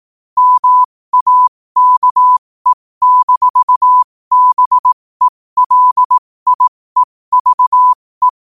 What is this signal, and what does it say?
Morse code, heard as a keyed tone, about 18 wpm, spaced as for 15 wpm, 1000 Hz. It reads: MAKE-BELIEVE
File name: qotdfiles / 15wpm